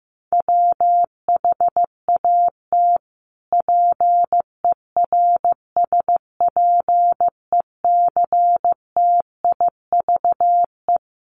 Morse code, heard as a keyed tone, about 15 wpm, 700 Hz. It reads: WHAT PERSPECTIVE
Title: Sentences from Top 1200 Words 15wpm